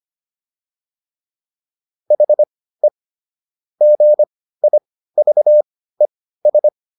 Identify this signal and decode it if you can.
Morse code, heard as a keyed tone, about 25 wpm, spaced as for 15 wpm, 600 Hz. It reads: HE GIVES